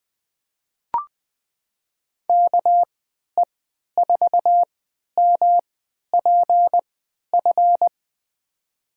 Morse code, heard as a keyed tone, about 20 wpm, spaced as for 11 wpm, 700 Hz. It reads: KE4MPF
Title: USA Call Signs - 3x Character Spacing - 20wpm